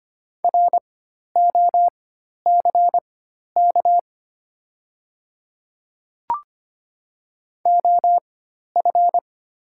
Morse code, heard as a keyed tone, about 25 wpm, spaced as for 12 wpm, 700 Hz. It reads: ROCK OF
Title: 4x Character Spacing - Top 500 Words - 25wpm